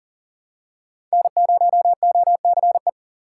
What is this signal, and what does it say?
Morse code, heard as a keyed tone, about 40 wpm, 700 Hz. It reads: N0OCE
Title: USA Call Signs - 5 Characters - 6x Character Spacing - 40wpm